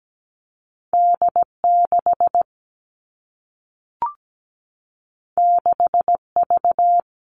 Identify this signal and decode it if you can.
Morse code, heard as a keyed tone, about 17 wpm, 700 Hz. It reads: D6 6V
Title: All Cut Numbers - 2-Digit 17wpm